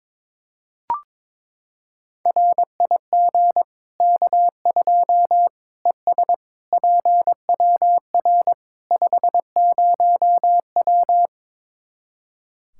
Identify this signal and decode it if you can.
Morse code, heard as a keyed tone, about 22 wpm, 700 Hz. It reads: RIG K2 ES PWR 50W
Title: Practice QSO - Exchange 2 - Info - 22wpm